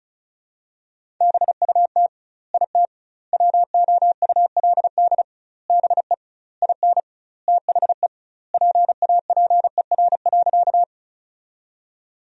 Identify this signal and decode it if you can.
Morse code, heard as a keyed tone, about 34 wpm, 700 Hz. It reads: BUT IT WOULD BE IN THE PAPER.
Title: 2x Word-Spacing - Sentences from Top 300 Words - 35wpm